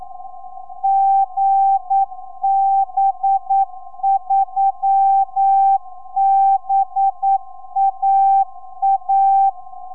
Morse code, heard as a keyed tone, about 9 wpm, 780 Hz. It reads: GB3BAA
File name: gb3baa 4m